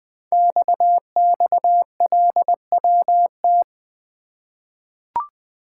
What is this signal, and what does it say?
Morse code, heard as a keyed tone, about 20 wpm, 700 Hz. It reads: XXLWT
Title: Code Groups - 5 Letters - 20wpm